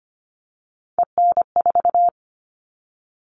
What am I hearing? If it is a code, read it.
Morse code, heard as a keyed tone, about 25 wpm, 700 Hz. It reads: EN4
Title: All Cut Numbers - 3-Digit 25wpm